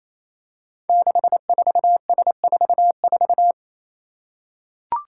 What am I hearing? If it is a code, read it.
Morse code, heard as a keyed tone, about 28 wpm, 700 Hz. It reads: 64S44
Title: Five Challenging Letter-Number Combinations 28wpm